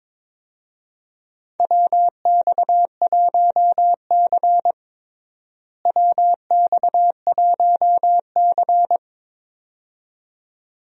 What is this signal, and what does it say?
Morse code, heard as a keyed tone, about 22 wpm, 700 Hz. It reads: WX1C WX1C
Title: USA Call Signs - 4 Characters - Repeated 2x - 22wpm